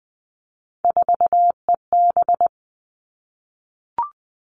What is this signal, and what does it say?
Morse code, heard as a keyed tone, about 20 wpm, 700 Hz. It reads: 4EB
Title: All Cut Numbers - 3-Digit 20wpm